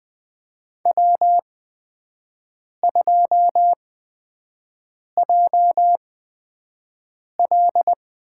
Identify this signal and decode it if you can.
Morse code, heard as a keyed tone, about 20 wpm, spaced as for 5 wpm, 700 Hz. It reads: W2JL
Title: USA Call Signs - 8x Character Spacing - 20wpm